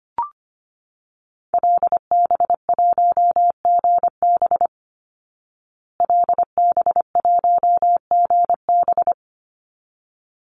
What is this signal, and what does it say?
Morse code, heard as a keyed tone, about 25 wpm, 700 Hz. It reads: LB1GB LB1GB
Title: Call Signs Repeated 3x - 25wpm